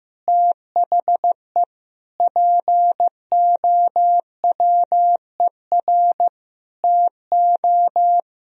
Morse code, heard as a keyed tone, about 16 wpm, 700 Hz. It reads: THE POWER TO
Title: Sentences from Top 1000 Words - Review Part 3 - 15wpm